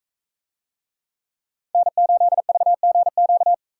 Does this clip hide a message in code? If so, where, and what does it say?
Yes: Morse code from 1.7 to 3.5 s, heard as a keyed tone, about 40 wpm, 700 Hz. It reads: N8VGQ